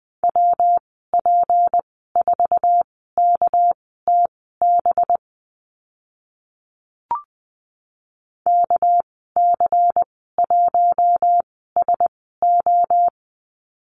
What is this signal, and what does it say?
Morse code, heard as a keyed tone, about 20 wpm, spaced as for 14 wpm, 700 Hz. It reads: WP4KTB KC1SO
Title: 245 Lesson 39g - Letter z - Call Signs 20wpm